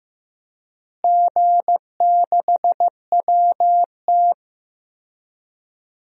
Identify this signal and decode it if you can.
Morse code, heard as a keyed tone, about 15 wpm, 700 Hz. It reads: G6WT
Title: DXCC-100 Call Signs - 4 Characters - Repeated 2x - 15wpm